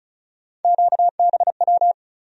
Morse code, heard as a keyed tone, about 36 wpm, 700 Hz. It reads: QBW
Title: USA Call Sign Postfixes - 3 Letters 35wpm